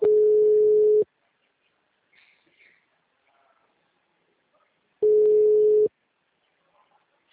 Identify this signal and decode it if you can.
Ringback tone (Europe, 425 Hz).
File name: out- -2002- - - .12730